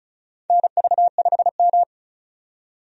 Morse code, heard as a keyed tone, about 36 wpm, 700 Hz. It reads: NV5M
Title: USA Call Signs - 4 Characters - 35wpm